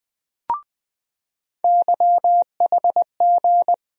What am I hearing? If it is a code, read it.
Morse code, heard as a keyed tone, about 20 wpm, 700 Hz. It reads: YHG
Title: USA Call Sign Postfixes - 3 Letters 20wpm